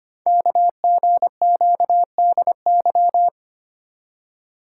Morse code, heard as a keyed tone, about 25 wpm, 700 Hz. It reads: KGQDY
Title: Code Groups - 5 Letters - 25wpm